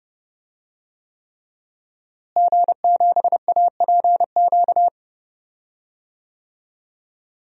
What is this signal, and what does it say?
Morse code, heard as a keyed tone, about 30 wpm, 700 Hz. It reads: G7APQ
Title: DXCC-100 Call Signs - 5 Characters - Repeated 3x - 30wpm